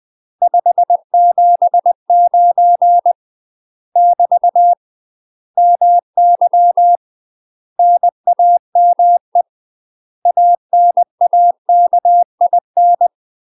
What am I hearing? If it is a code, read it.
Morse code, heard as a keyed tone, about 20 wpm, 700 Hz. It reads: 579 <BT> MY NAME ANAKIN
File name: UK MASTER QSOs TRAINING / 20WPM / UK MASTER QSOs 0023